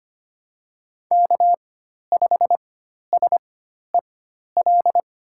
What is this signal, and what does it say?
Morse code, heard as a keyed tone, about 25 wpm, spaced as for 12 wpm, 700 Hz. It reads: K5SEL
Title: USA Call Signs - 5 Characters - 4x Character Spacing - 25wpm